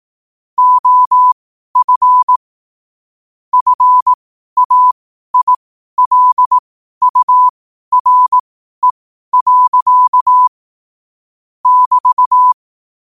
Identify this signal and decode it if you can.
Morse code, heard as a keyed tone, about 18 wpm, spaced as for 12 wpm, 1000 Hz. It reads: OF FAILURE. <BT>
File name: qotdfiles / 12wpm